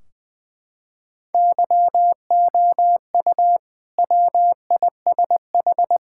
Morse code, heard as a keyed tone, about 21 wpm, 700 Hz. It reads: YOU WISH
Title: Sentences from Top 1600 Words 20wpm